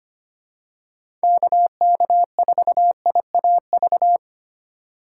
Morse code, heard as a keyed tone, about 25 wpm, 700 Hz. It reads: KK4IAV
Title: USA Call Signs - 3x Character Spacing - 25wpm